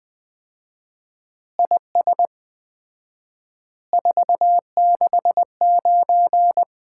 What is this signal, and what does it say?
Morse code, heard as a keyed tone, about 20 wpm, 700 Hz. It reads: IS 469
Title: Common QSO Phrases - 4x Word-Spacing - 20wpm